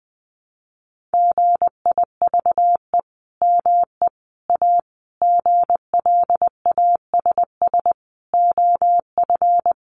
Morse code, heard as a keyed tone, about 20 wpm, 700 Hz. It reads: GIVE ME A GLASS OF
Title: Sentences from Top 1500 Words 20wpm